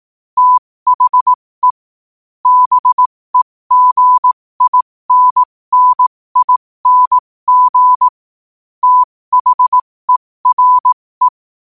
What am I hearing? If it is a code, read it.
Morse code, heard as a keyed tone, about 18 wpm, spaced as for 15 wpm, 1000 Hz. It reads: THE BEGINNING THERE
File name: 15wpm